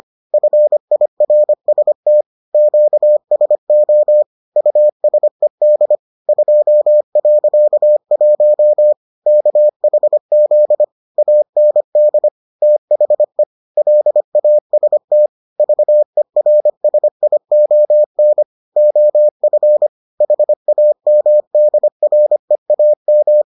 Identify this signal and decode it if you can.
Morse code, heard as a keyed tone, about 25 wpm, 600 Hz. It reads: FIRST QSO USED 2.1 KHZ AND THE LAST VERSION OF HAMDREAM